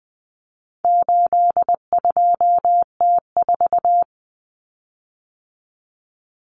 Morse code, heard as a keyed tone, about 20 wpm, 700 Hz. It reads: 82T4
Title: Common Cut Numbers - 4-Digit 20wpm